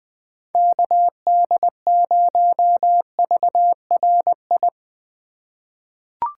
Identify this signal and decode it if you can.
Morse code, heard as a keyed tone, about 20 wpm, 700 Hz. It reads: KD0VRI
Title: USA Call Signs - 6 Characters - 4x Character Spacing - 20wpm